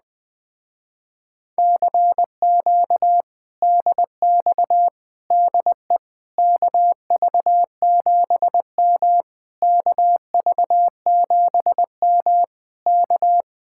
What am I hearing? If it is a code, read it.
Morse code, heard as a keyed tone, about 20 wpm, 700 Hz. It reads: CQ DX DE KV7M KV7M K
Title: Speed-Racing Common QSO Phrases - 17wpm